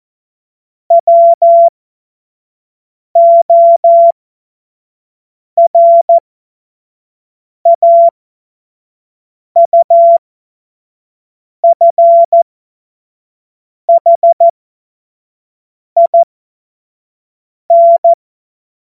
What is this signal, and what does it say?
Morse code, heard as a keyed tone, about 14 wpm, spaced as for 5 wpm, 680 Hz. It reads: WORAUFHIN
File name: Morsekurs / Witze / Witz 06. Dezember 2022 25 F70